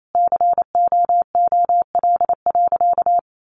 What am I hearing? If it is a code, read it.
Morse code, heard as a keyed tone, about 28 wpm, 700 Hz. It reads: COOL.
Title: Sentences from Top 1200 Words 28wpm